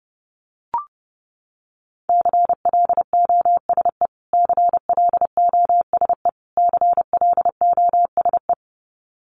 Morse code, heard as a keyed tone, about 30 wpm, 700 Hz. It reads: CLOSE CLOSE CLOSE
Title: Top 300 Words Repeated 30wpm